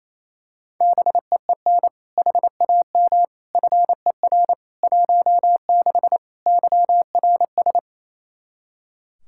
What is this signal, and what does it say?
Morse code, heard as a keyed tone, about 27 wpm, 700 Hz. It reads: BEEN HAM FER 16 YRS
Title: K1USN SST QSO Practice 28wpm